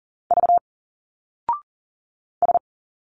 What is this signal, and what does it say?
Morse code, heard as a keyed tone, about 40 wpm, 700 Hz. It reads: V S
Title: Single Letters 40wpm